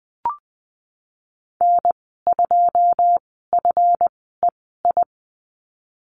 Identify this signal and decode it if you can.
Morse code, heard as a keyed tone, about 20 wpm, spaced as for 15 wpm, 700 Hz. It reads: N2FEI
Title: 147 Lesson 24g - Letter y - Call Signs 20wpm